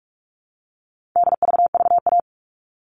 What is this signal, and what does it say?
Morse code, heard as a keyed tone, about 44 wpm, 700 Hz. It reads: DVVA
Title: All Cut Numbers - 4-Digit 45wpm